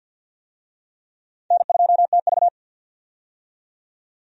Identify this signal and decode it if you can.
Morse code, heard as a keyed tone, about 53 wpm, 700 Hz. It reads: N2TV
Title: USA Call Signs - 4 Characters - 50wpm